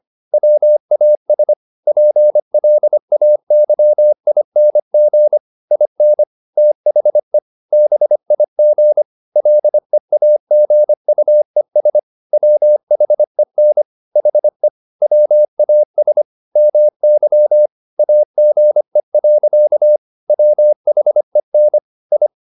Morse code, heard as a keyed tone, about 25 wpm, 600 Hz. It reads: WAS PLAYING IN THE BIG LEAGUES WHEN HE WAS MY AGE. WHEN I
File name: the old man and the sea part 1 25wpm